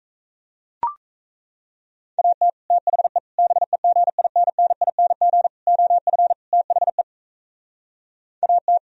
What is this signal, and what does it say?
Morse code, heard as a keyed tone, about 37 wpm, 700 Hz. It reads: AT THE BEGINNING OF THE AT